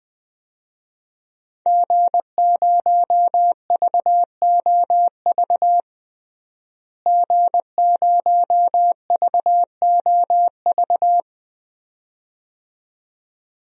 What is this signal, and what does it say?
Morse code, heard as a keyed tone, about 20 wpm, 700 Hz. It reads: G0VOV G0VOV
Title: DXCC-100 Call Signs - 5 Characters - Repeated 2x - 20wpm